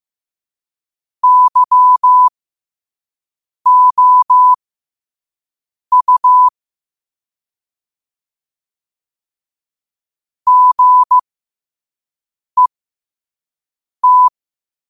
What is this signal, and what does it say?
Morse code, heard as a keyed tone, about 15 wpm, spaced as for 5 wpm, 1000 Hz. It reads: YOU GET
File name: qotdfiles / 5wpm